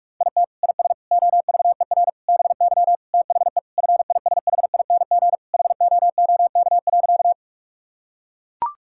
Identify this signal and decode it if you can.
Morse code, heard as a keyed tone, about 41 wpm, 700 Hz. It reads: IT IS OVER BY THE FISHING HOOK.
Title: Sentences from Top 1600 Words 45wpm